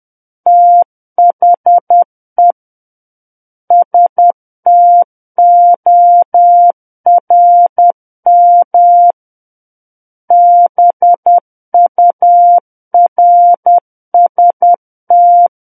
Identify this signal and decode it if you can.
Morse code, heard as a keyed tone, about 10 wpm, 700 Hz. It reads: THE STORM BURST
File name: Book One / 01-Chapter 1 - 10 WPM